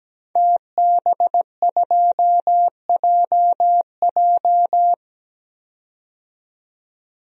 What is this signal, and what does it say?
Morse code, heard as a keyed tone, about 17 wpm, 700 Hz. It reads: TB2JJ
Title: DXCC-100 Call Signs - 5 Characters - 17wpm